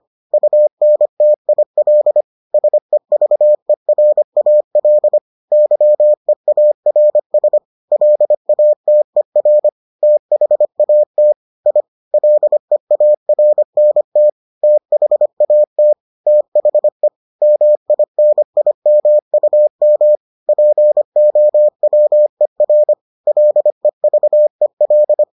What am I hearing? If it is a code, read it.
Morse code, heard as a keyed tone, about 25 wpm, 600 Hz. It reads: UNTIL SEVERAL YEARS LATER THAT I LEARNT THAT THE MINIMUM POWER LEVEL